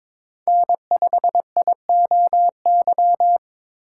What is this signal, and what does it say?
Morse code, heard as a keyed tone, about 22 wpm, 700 Hz. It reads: N5IOY
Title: USA Call Signs - 5 Characters - 8x Character Spacing - 22wpm